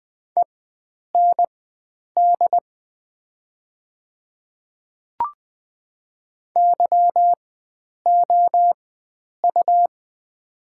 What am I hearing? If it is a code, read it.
Morse code, heard as a keyed tone, about 20 wpm, spaced as for 9 wpm, 700 Hz. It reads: END YOU
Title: 4x Character Spacing - Top 300 Words - 20wpm